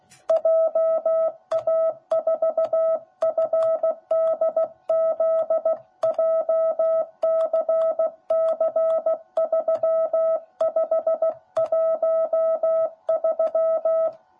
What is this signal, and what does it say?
Morse code, heard as a keyed tone, about 16 wpm, 660 Hz. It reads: JA4FDZ JCC3513
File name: jh8cbh / 2021cw / cw10b